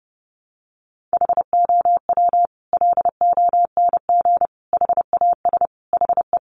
Morse code, heard as a keyed tone, about 30 wpm, 700 Hz. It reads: HOW LONG HAS HE
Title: Sentences from Top 100 Words 30wpm